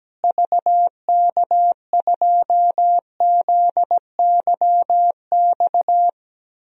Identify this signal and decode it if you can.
Morse code, heard as a keyed tone, about 17 wpm, 700 Hz. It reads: VK2ZYX
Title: DXCC-100 Call Signs - 6 Characters - Repeated 3x - 17wpm